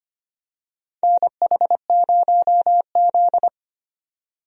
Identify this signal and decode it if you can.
Morse code, heard as a keyed tone, about 25 wpm, 700 Hz. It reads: NH0Z
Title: USA Call Signs - 4 Characters - 25wpm